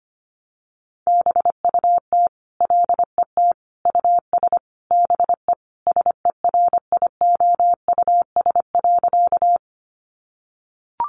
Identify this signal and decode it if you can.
Morse code, heard as a keyed tone, about 25 wpm, 700 Hz. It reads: BUT LET US BE SERIOUS.
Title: Sentences from Top 1100 Words 25wpm